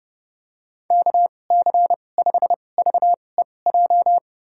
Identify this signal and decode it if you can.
Morse code, heard as a keyed tone, about 32 wpm, spaced as for 22 wpm, 700 Hz. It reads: KC5VEJ
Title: USA Call Signs - 6 Characters - 2x Character Spacing - 30wpm